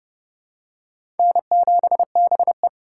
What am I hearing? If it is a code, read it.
Morse code, heard as a keyed tone, about 30 wpm, 700 Hz. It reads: N7BE
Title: USA Call Signs - 2x Character Spacing - 30wpm